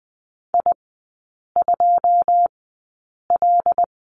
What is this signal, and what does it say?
Morse code, heard as a keyed tone, about 20 wpm, spaced as for 8 wpm, 700 Hz. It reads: I2L
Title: 100 Lesson 17e - Letter u - Three Characters 20wpm